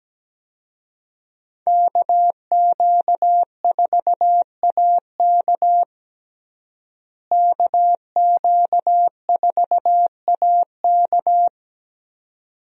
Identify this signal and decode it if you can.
Morse code, heard as a keyed tone, about 17 wpm, 700 Hz. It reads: KQ4AK KQ4AK
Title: USA Call Signs - 5 Characters - Repeated 2x - 17wpm